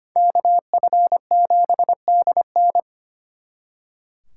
Morse code, heard as a keyed tone, about 31 wpm, 700 Hz. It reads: KF7DN